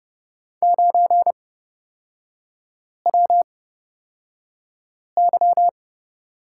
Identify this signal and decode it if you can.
Morse code, heard as a keyed tone, about 30 wpm, spaced as for 5 wpm, 700 Hz. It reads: 9WY
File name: Lesson 36e - BT - Three Characters 30wpm